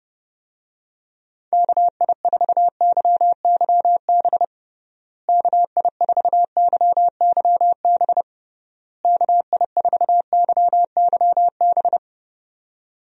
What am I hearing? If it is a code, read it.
Morse code, heard as a keyed tone, about 30 wpm, 700 Hz. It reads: KI4YYB KI4YYB KI4YYB
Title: USA Call Signs - 6 Characters - Repeated 3x - 30wpm